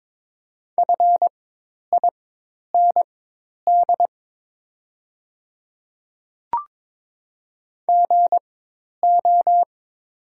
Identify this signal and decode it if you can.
Morse code, heard as a keyed tone, about 22 wpm, spaced as for 10 wpm, 700 Hz. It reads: FIND GO
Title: 4x Character Spacing - Top 100 Words - 22wpm